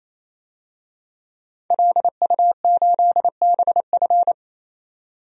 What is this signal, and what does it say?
Morse code, heard as a keyed tone, about 28 wpm, 700 Hz. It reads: LU8BF
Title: DXCC-100 Call Signs - 5 Characters - 28wpm